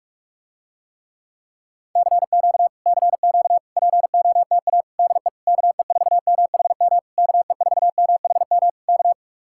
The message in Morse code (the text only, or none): CQ CQ POTA DE KE4MHM KE4MHM K